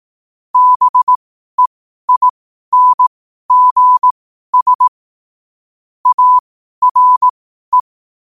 Morse code, heard as a keyed tone, about 18 wpm, spaced as for 12 wpm, 1000 Hz. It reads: BEINGS ARE
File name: 2026-01-07-12wpm